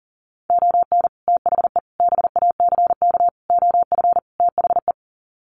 Morse code, heard as a keyed tone, about 40 wpm, 700 Hz. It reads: ON THE BACK OF THE
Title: Sets of 5 Words 40wpm